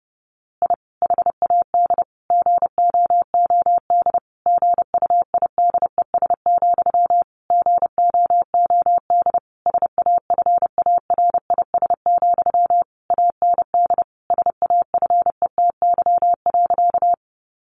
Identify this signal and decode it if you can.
Morse code, heard as a keyed tone, about 30 wpm, 700 Hz. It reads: I HAD GOOD GUIDES, GOOD SAFARIS, AND SAFETY.
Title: A Sound Of Thunder 30wpm